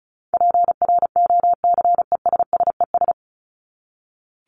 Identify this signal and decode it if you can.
Morse code, heard as a keyed tone, about 34 wpm, 700 Hz. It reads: PROCESSES
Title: ES:S Suffix - Top 100 Words with Base Words 35wpm